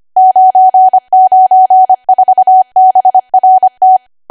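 Morse code, heard as a keyed tone, about 25 wpm, 750 Hz. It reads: 994BRT
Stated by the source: ARRL 25 WPM code practice archive.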